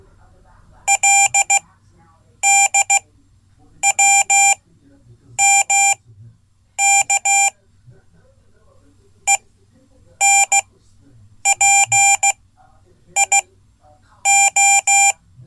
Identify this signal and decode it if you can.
Morse code, heard as a keyed tone, about 16 wpm, spaced as for 8 wpm, 790 Hz. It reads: LDWMK ENPIO
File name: Code / 5min-c